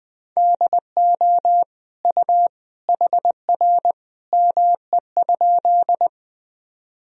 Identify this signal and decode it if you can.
Morse code, heard as a keyed tone, about 21 wpm, 700 Hz. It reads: DO U HR ME?
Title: Sentences from Top 300 Words and Abbreviations 20wpm